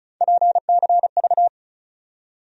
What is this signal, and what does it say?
Morse code, heard as a keyed tone, about 36 wpm, 700 Hz. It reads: PCV
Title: USA Call Sign Postfixes - 3 Letters 35wpm